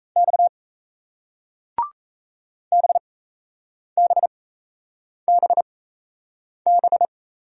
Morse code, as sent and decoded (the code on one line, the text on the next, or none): -..- / -... -... -... -...
X BBBB